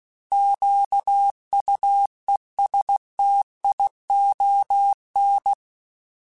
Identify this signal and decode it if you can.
Morse code, heard as a keyed tone, about 16 wpm, 800 Hz. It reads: QUESTION
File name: Misc / Misc-mp3s / Misc @15